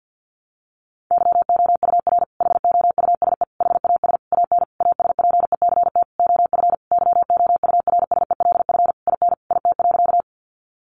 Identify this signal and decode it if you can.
Morse code, heard as a keyed tone, about 50 wpm, 700 Hz. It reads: YOUR HOUSE HAS AN ASPECT OF YOURSELF IN IT.
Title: Sentences from Top 1300 Words 50wpm